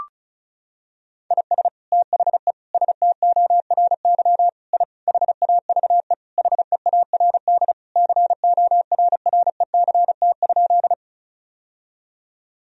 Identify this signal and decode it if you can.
Morse code, heard as a keyed tone, about 36 wpm, 700 Hz. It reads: IS THE STORY I HAVE HEARD CORRECT?
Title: Sentences from Top 1000 Words - Review Part 2 - 35wpm